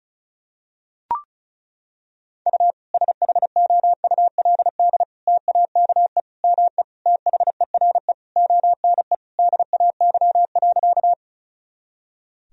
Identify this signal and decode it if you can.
Morse code, heard as a keyed tone, about 36 wpm, 700 Hz. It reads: U SHOULD TAKE ME THERE ONE DAY.
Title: Sentences from Top 300 Words and Abbreviations 35wpm